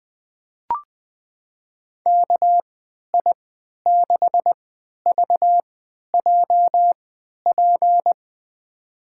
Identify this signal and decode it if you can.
Morse code, heard as a keyed tone, about 20 wpm, spaced as for 11 wpm, 700 Hz. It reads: KI6VJP
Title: USA Call Signs - 6 Characters - 3x Character Spacing - 20wpm